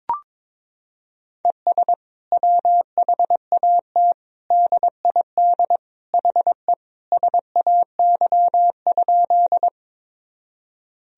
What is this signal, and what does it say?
Morse code, heard as a keyed tone, about 22 wpm, 700 Hz. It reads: ES WHAT DID HE SAY?
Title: Sentences from Top 200 Words and Abbreviations 22wpm